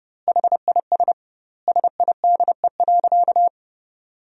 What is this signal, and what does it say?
Morse code, heard as a keyed tone, about 32 wpm, 700 Hz. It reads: HIS SIDE.
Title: 2x Word-Spacing - Sentences from Top 400 Words - 30wpm